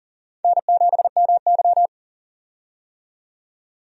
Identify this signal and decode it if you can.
Morse code, heard as a keyed tone, about 40 wpm, 690 Hz. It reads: N7MY
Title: USA Call Signs - 4 Characters - 40wpm